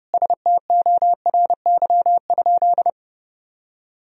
Morse code, heard as a keyed tone, about 32 wpm, 700 Hz. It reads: STORY?
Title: Sentences from Top 1000 Words - Review Part 1 - 30wpm